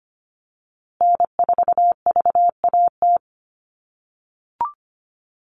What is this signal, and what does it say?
Morse code, heard as a keyed tone, about 25 wpm, 700 Hz. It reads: N4VAT
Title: USA Call Signs Repeated 3x 25wpm